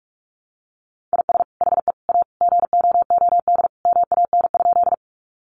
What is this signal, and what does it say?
Morse code, heard as a keyed tone, about 44 wpm, 700 Hz. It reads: IS HE A GOOD MAN?
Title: Sentences from Top 200 Words 45wpm